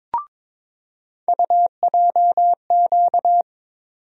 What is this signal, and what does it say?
Morse code, heard as a keyed tone, about 22 wpm, 700 Hz. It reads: UJQ
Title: USA Call Sign Postfixes - 3 Letters 22wpm